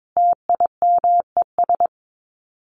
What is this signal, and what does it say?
Morse code, heard as a keyed tone, about 22 wpm, 700 Hz. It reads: TIMES
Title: Sets of 5 Words 22wpm